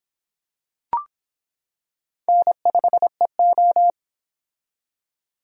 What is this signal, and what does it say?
Morse code, heard as a keyed tone, about 25 wpm, 700 Hz. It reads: N5EO